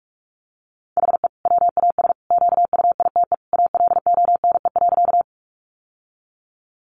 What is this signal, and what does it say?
Morse code, heard as a keyed tone, about 44 wpm, 700 Hz. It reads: HE WAS QUITE ALONE.
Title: Sentences from Top 1000 Words 45wpm